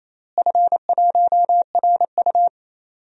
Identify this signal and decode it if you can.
Morse code, heard as a keyed tone, about 28 wpm, 700 Hz. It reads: F1RU
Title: DXCC-100 Call Signs - 4 Characters - Repeated 3x - 28wpm